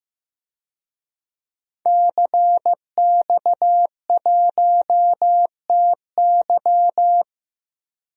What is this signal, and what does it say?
Morse code, heard as a keyed tone, about 15 wpm, 700 Hz. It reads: CX1TY
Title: DXCC-100 Call Signs - 5 Characters - 15wpm